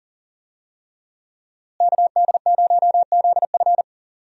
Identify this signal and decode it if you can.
Morse code, heard as a keyed tone, about 40 wpm, 700 Hz. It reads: KD0ZF
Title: USA Call Signs - 5 Characters - Repeated 3x - 40wpm